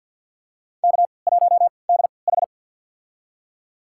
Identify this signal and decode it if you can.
Morse code, heard as a keyed tone, about 49 wpm, spaced as for 28 wpm, 700 Hz. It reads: K1DH
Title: USA Call Signs - 4 Characters - 3x Character Spacing - 50wpm